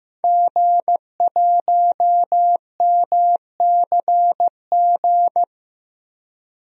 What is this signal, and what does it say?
Morse code, heard as a keyed tone, about 16 wpm, 700 Hz. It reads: G1MCG
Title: DXCC-100 Call Signs - 5 Characters - Repeated 3x - 15wpm